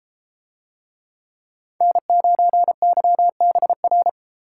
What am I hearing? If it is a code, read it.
Morse code, heard as a keyed tone, about 32 wpm, 700 Hz. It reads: N9YBR